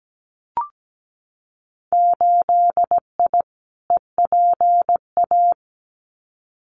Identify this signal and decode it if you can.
Morse code, heard as a keyed tone, about 17 wpm, 700 Hz. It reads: 8I EPA
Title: Winter Field Day Exchanges Encore 17wpm